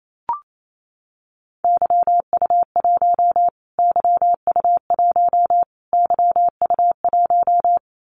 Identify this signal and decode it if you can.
Morse code, heard as a keyed tone, about 28 wpm, 700 Hz. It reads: YU1 YU1 YU1
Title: Callsign Prefixes for Top 25 DXCC - Repeated 3x - 28wpm